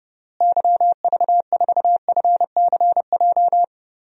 Morse code, heard as a keyed tone, about 30 wpm, 700 Hz. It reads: YV4FCJ
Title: DXCC-100 Call Signs - 6 Characters - Repeated 3x - 30wpm